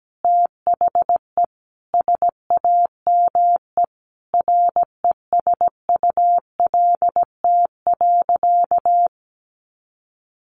Morse code, heard as a keyed tone, about 17 wpm, 700 Hz. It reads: THE SAME RESULT.
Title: Sentences from Top 700 Words 17wpm